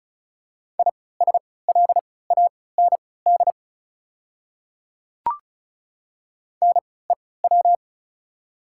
Morse code, heard as a keyed tone, about 36 wpm, spaced as for 20 wpm, 690 Hz. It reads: ISLAND NEW
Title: 3x Character Spacing - Top 500 Words - 35wpm